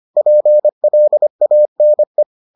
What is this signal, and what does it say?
Morse code, heard as a keyed tone, about 25 wpm, 600 Hz. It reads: PLANE